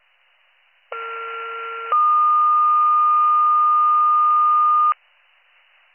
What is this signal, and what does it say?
Two-tone page, heard as a one-second tone, then a three-second tone, about 1370 then 1220 Hz. It is heard from 0.9 to 4.9 s.